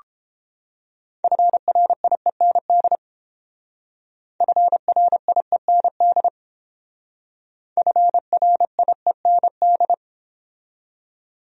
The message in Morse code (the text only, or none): FRIEND FRIEND FRIEND